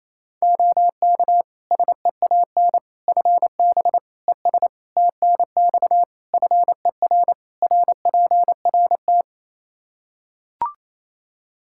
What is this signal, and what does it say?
Morse code, heard as a keyed tone, about 28 wpm, 700 Hz. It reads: OK SEAN FB ES TNX FER RPRT
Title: Practice QSO - Exchange 2 - Info - 28wpm